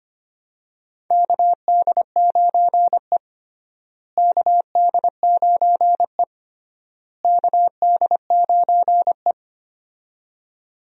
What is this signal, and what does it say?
Morse code, heard as a keyed tone, about 25 wpm, 700 Hz. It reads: KD9E KD9E KD9E
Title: USA Call Signs - 4 Characters - Repeated 3x - 25wpm